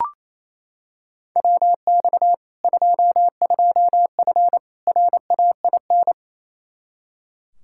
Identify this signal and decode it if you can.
Morse code, heard as a keyed tone, about 28 wpm, 700 Hz. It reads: WX 22F RAIN
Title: Practice QSO - Exchange 2 - Info - 28wpm